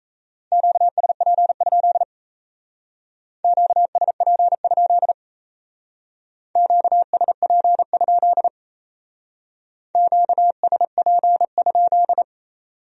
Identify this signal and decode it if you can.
Morse code, heard as a keyed tone, about 33 wpm, 700 Hz. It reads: QSP? QSP? QSP? QSP?
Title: Speed-Racing Q-Codes - 28wpm